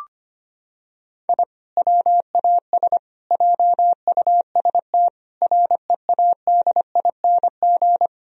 Morse code, heard as a keyed tone, about 25 wpm, 700 Hz. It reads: I WAS JUST READING
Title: Sentences from Top 1000 Words - Review Part 3 - 25wpm